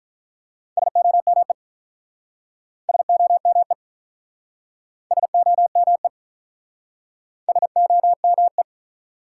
Morse code, heard as a keyed tone, about 46 wpm, 700 Hz. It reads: SOME SOME SOME SOME